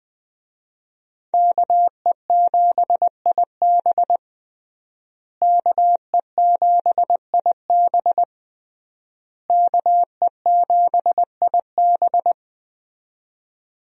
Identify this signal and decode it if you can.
Morse code, heard as a keyed tone, about 20 wpm, 700 Hz. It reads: KE7IB KE7IB KE7IB
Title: USA Call Signs - 5 Characters - Repeated 3x - 20wpm